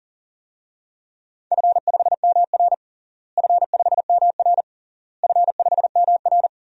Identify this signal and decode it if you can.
Morse code, heard as a keyed tone, about 40 wpm, 700 Hz. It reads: F5MR F5MR F5MR
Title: DXCC-100 Call Signs - 4 Characters - Repeated 3x - 40wpm